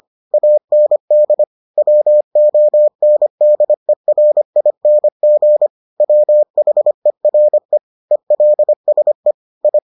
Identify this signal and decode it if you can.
Morse code, heard as a keyed tone, about 25 wpm, 600 Hz. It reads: AND WONDERING WHERE ELSE I